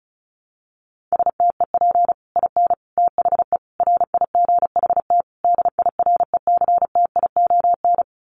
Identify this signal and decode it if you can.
Morse code, heard as a keyed tone, about 34 wpm, 700 Hz. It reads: STEP IN THE RIGHT DIRECTION
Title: Sets of 5 Words Encore 35wpm